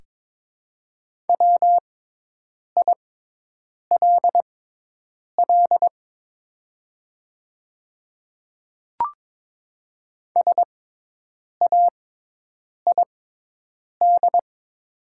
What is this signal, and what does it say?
Morse code, heard as a keyed tone, about 22 wpm, spaced as for 8 wpm, 700 Hz. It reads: WILL SAID
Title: 6x Character Spacing - Top 200 Words - 22wpm